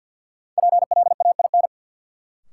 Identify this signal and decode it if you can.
Morse code, heard as a keyed tone, about 53 wpm, 700 Hz. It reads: PLAIN
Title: Sentences from Top 400 Words - New Word Drill - 50wpm